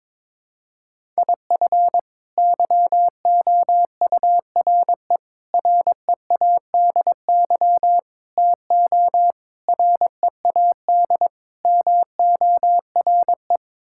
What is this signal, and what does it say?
Morse code, heard as a keyed tone, about 22 wpm, 700 Hz. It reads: IF YOURE READY TO READ MORE
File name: practice / 22wpm / Sets of 6 Words - 22wpm